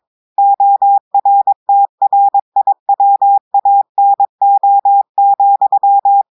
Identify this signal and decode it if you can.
Morse code, heard as a keyed tone, about 22 wpm, 800 Hz. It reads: ORTRIWANO,
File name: audios / portugais / portugais texte en clair 1 vitesse 22 mots minutes